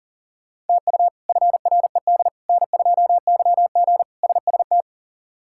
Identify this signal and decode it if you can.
Morse code, heard as a keyed tone, about 40 wpm, 700 Hz. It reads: TU FRED N2YG SST
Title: K1USN SST QSO Practice - 40wpm